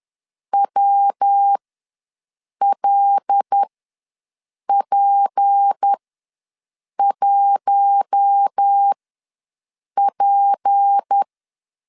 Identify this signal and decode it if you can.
Morse code, heard as a keyed tone, about 10 wpm, spaced as for 6 wpm, 800 Hz. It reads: WLP1P
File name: LEZIONI / 6 PKX0 / 6 P ws 10wpm 3751